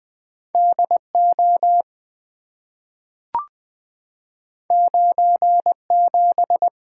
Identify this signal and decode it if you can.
Morse code, heard as a keyed tone, about 20 wpm, 700 Hz. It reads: D O 9 7
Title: Two Letter-Number Combinations - 20wpm